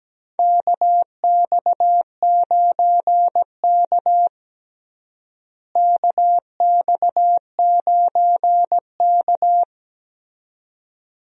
Morse code, heard as a keyed tone, about 17 wpm, 700 Hz. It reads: KX9K KX9K
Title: USA Call Signs - 4 Characters - Repeated 2x - 17wpm